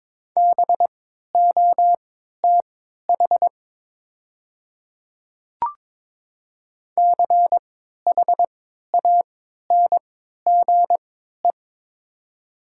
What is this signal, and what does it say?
Morse code, heard as a keyed tone, about 22 wpm, spaced as for 12 wpm, 700 Hz. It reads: BOTH CHANGE
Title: 3x Character Spacing - Top 300 Words - 22wpm